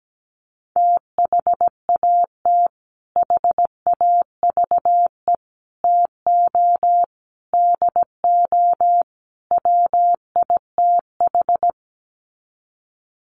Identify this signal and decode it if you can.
Morse code, heard as a keyed tone, about 17 wpm, 700 Hz. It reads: THAT HAVE TO DO WITH
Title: Sets of 5 Words Encore2 17wpm